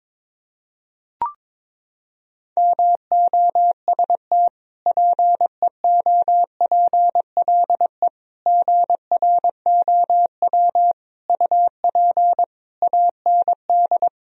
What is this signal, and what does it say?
Morse code, heard as a keyed tone, about 22 wpm, 700 Hz. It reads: MOST PEOPLE GROW UP AND
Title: Sentences from Top 1000 Words - Review Part 1 - 22wpm